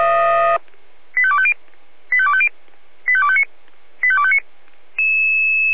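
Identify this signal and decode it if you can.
Five-tone selective call (ZVEI), heard as quick runs of five tones, tones of about 70 ms. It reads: 85250 85250 85259 85259
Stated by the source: ZVEI five-tone call received by SDR.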